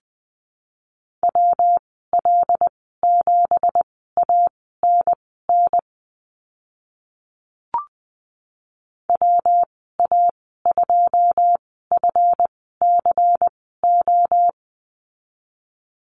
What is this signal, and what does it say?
Morse code, heard as a keyed tone, about 20 wpm, spaced as for 14 wpm, 700 Hz. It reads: WL7ANN WA2FCO
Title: 167 Lesson 27g - Number 7 - Call Signs 20wpm